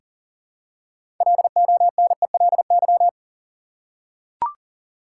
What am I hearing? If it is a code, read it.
Morse code, heard as a keyed tone, about 40 wpm, 700 Hz. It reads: LONELY
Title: Six Letter Words - 40wpm